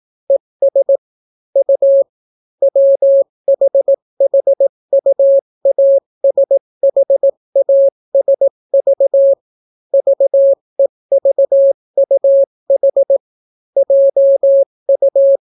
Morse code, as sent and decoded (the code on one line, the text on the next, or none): . ... / ..- / .-- .... .... ..- .- ... .... .- ... ...- / ...- . ...- ..- .... / .--- ..-
ES U WHHUASHASV VEVUH JU